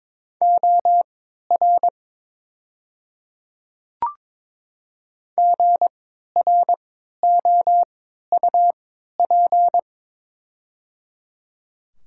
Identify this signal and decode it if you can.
Morse code, heard as a keyed tone, about 23 wpm, spaced as for 13 wpm, 700 Hz. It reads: OR GROUP
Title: 3x Character Spacing - Top 500 Words - 22wpm